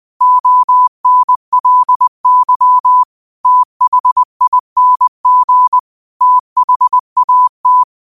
Morse code, heard as a keyed tone, about 20 wpm, 1000 Hz. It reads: ONLY THING THAT
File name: qotdfiles / 20wpm